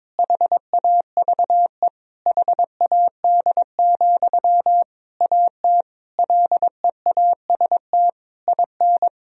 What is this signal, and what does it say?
Morse code, heard as a keyed tone, about 22 wpm, 700 Hz. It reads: HAVE HAD, AT LEAST IN
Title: Sentences from Top 1000 Words - Review Part 4 - 22wpm